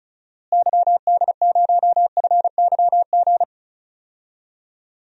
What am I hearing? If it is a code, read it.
Morse code, heard as a keyed tone, about 34 wpm, 700 Hz. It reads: YD0FYG
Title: DXCC-100 Call Signs - 6 Characters - Repeated 2x - 35wpm